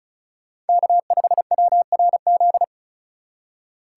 Morse code, heard as a keyed tone, about 36 wpm, 700 Hz. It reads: K5WRZ
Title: USA Call Signs - 5 Characters - 2x Character Spacing - 35wpm